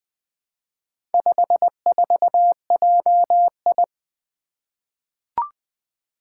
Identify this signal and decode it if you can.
Morse code, heard as a keyed tone, about 20 wpm, 700 Hz. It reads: 54JI
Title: Four Letter-Number Combinations - 20wpm